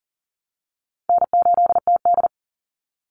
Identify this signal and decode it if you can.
Morse code, heard as a keyed tone, about 40 wpm, 700 Hz. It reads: N8TD
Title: Call Signs Repeated 3x - 40wpm